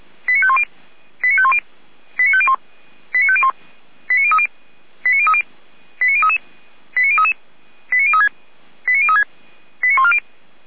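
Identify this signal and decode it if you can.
Five-tone selective call (ZVEI), heard as quick runs of five tones, tones of about 70 ms. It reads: 89510 89510 89501 89501 89030 89030 89033 89033 89036 89036 89140
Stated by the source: ZVEI five-tone selective call 89042 received by SDR.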